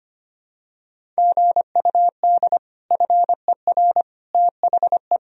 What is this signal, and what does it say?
Morse code, heard as a keyed tone, about 25 wpm, 700 Hz. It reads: GUD FER THE
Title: Sentences from Top 300 Words and Abbreviations 25wpm